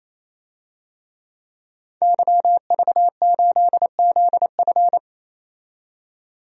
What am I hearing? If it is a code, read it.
Morse code, heard as a keyed tone, about 28 wpm, 700 Hz. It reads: YV8ZF
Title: DXCC-100 Call Signs - 5 Characters - 28wpm